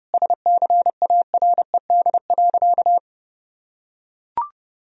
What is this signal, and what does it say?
Morse code, heard as a keyed tone, about 30 wpm, 700 Hz. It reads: SCARED.
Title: Sentences from Top 1000 Words - Review Part 4 - 30wpm